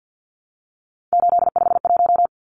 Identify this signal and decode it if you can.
Morse code, heard as a keyed tone, about 50 wpm, 700 Hz. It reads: 851
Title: Numbers - 3-Digit 50wpm